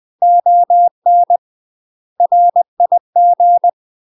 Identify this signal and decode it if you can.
Morse code, heard as a keyed tone, about 20 wpm, 700 Hz. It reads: ON RIG
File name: UK MASTER QSOs TRAINING / 20WPM / UK MASTER QSOs 0003